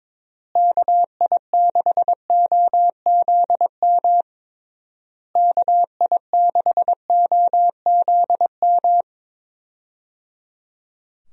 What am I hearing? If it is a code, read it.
Morse code, heard as a keyed tone, about 22 wpm, 700 Hz. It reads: KI6OZM KI6OZM
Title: USA Call Signs - 6 Characters - Repeated 3x - 22wpm